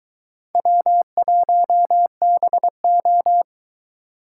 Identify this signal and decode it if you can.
Morse code, heard as a keyed tone, about 23 wpm, 700 Hz. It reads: W1BO